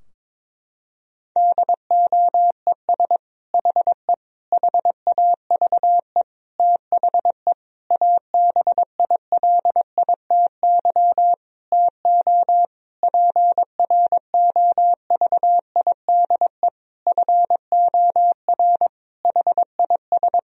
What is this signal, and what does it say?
Morse code, heard as a keyed tone, about 22 wpm, 700 Hz. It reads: DOES HE HAVE THE ABILITY TO PROVIDE FOR HIS
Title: Sentences from Top 1000 Words - Review Part 3 - 22wpm